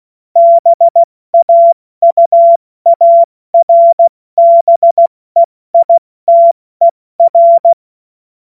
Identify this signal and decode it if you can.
Morse code, heard as a keyed tone, about 16 wpm, 680 Hz. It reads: BAUARBEITER
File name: Morsekurs / Witze / Witz 20 Februar 2024 70 F80